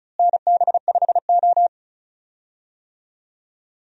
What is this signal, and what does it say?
Morse code, heard as a keyed tone, about 36 wpm, 700 Hz. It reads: NB5O
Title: USA Call Signs - 4 Characters - Repeated 3x - 35wpm